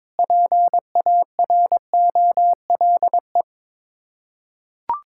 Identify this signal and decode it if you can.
Morse code, heard as a keyed tone, about 22 wpm, 700 Hz. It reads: PAROLE
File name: practice / 22wpm / Sets of 8 Words - 22wpm